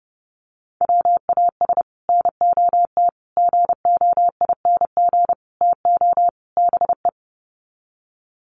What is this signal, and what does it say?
Morse code, heard as a keyed tone, about 30 wpm, 700 Hz. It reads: WAS NOT GOING TO BE
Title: Sets of 5 Words Encore 30wpm